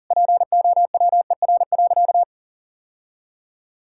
Morse code, heard as a keyed tone, about 40 wpm, 700 Hz. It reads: POWER.
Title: Sentences from Top 1000 Words - Review Part 6 - 40wpm